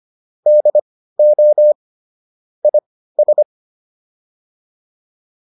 Morse code, heard as a keyed tone, about 25 wpm, spaced as for 15 wpm, 600 Hz. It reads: DO IS
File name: WD404 15